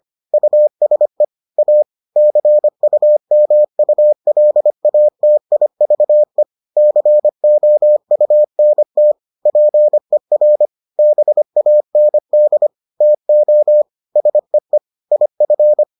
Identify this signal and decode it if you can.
Morse code, heard as a keyed tone, about 25 wpm, 600 Hz. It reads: USE A CUMULATIVE COUNT PER BAND TO SEE IF